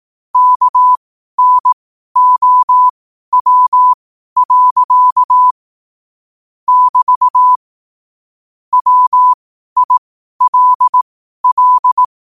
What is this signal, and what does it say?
Morse code, heard as a keyed tone, about 18 wpm, spaced as for 12 wpm, 1000 Hz. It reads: KNOW. <BT> WILL
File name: qotdfiles / 12wpm